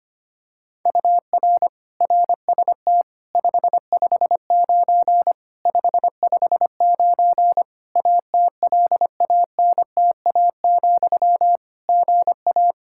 Morse code, heard as a keyed tone, about 25 wpm, 700 Hz. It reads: UR RST 559 559 ATLANTA, GA
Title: K1USN SST QSO Practice 25wpm